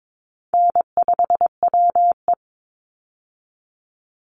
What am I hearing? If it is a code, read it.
Morse code, heard as a keyed tone, about 22 wpm, 700 Hz. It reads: N5WE
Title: Easy Call Signs - 22wpm